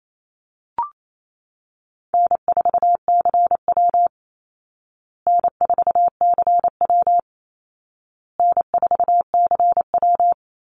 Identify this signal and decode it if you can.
Morse code, heard as a keyed tone, about 28 wpm, 700 Hz. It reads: N4CW N4CW N4CW
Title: Call Signs Repeated 2x - 28wpm